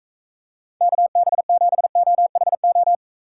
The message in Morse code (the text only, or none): KB7OHO